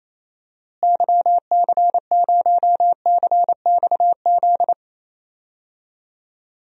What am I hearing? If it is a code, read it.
Morse code, heard as a keyed tone, about 28 wpm, 700 Hz. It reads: YC0CXZ
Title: DXCC-100 Call Signs - 6 Characters - Repeated 2x - 28wpm